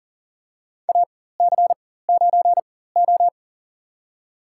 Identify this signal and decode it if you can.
Morse code, heard as a keyed tone, about 40 wpm, spaced as for 19 wpm, 700 Hz. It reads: AC9O
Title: USA Call Signs - 4 Characters - 4x Character Spacing - 40wpm